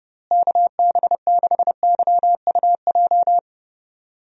Morse code, heard as a keyed tone, about 30 wpm, 700 Hz. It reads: KB6YUJ
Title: USA Call Signs - 6 Characters - Repeated 2x - 30wpm